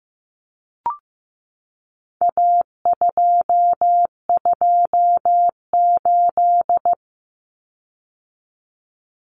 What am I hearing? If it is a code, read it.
Morse code, heard as a keyed tone, about 15 wpm, 700 Hz. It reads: A228
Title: Common Cut Numbers - 4-Digit 15wpm